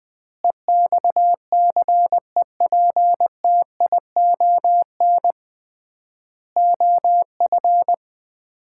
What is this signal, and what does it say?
Morse code, heard as a keyed tone, about 20 wpm, 700 Hz. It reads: EXCEPTION OF
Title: 3x Word-Spacing - Sets of 4 Words - 20wpm